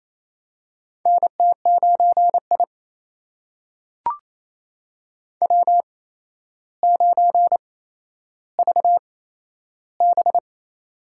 Morse code, heard as a keyed tone, about 28 wpm, spaced as for 12 wpm, 700 Hz. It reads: NT9I W9VB
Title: USA Call Signs - 4 Characters - 8x Character Spacing - 28wpm